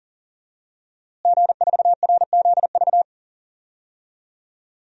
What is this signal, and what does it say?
Morse code, heard as a keyed tone, about 40 wpm, 700 Hz. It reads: G4RZV
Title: DXCC-100 Call Signs - 5 Characters - Repeated 3x - 40wpm